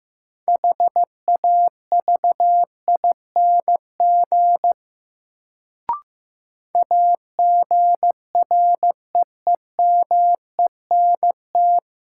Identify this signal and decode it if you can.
Morse code, heard as a keyed tone, about 16 wpm, 700 Hz. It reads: HAVING AGREEMENT
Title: Sets of 5 Words Encore - New Word Drill - 15wpm